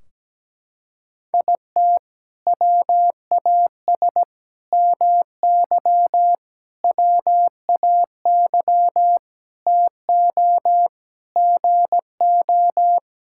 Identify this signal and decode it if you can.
Morse code, heard as a keyed tone, about 17 wpm, 700 Hz. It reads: IT WAS MY WAY TO GO
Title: Sentences from Top 1600 Words 17wpm